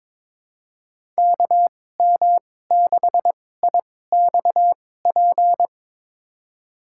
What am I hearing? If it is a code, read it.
Morse code, heard as a keyed tone, about 22 wpm, spaced as for 16 wpm, 700 Hz. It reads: KM6IXP
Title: USA Call Signs - 6 Characters - 2x Character Spacing - 22wpm